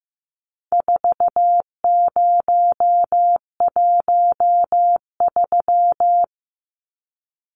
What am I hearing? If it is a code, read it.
Morse code, heard as a keyed tone, about 15 wpm, 700 Hz. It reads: 4013
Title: Numbers - 4-Digit 15wpm